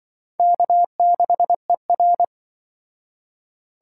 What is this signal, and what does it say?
Morse code, heard as a keyed tone, about 24 wpm, 700 Hz. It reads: K6ER
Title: Speed Racing - USA Call Signs 20wpm